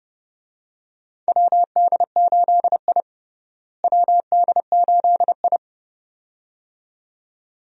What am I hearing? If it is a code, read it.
Morse code, heard as a keyed tone, about 32 wpm, 700 Hz. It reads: WD8I WD8I
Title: USA Call Signs - 4 Characters - Repeated 2x - 30wpm